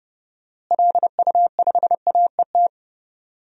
Morse code, heard as a keyed tone, about 29 wpm, 700 Hz. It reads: LU5AET
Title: DXCC-100 Call Signs - 6 Characters - Repeated 2x - 30wpm